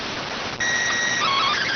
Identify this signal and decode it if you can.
SSTV (slow-scan television), PD 290.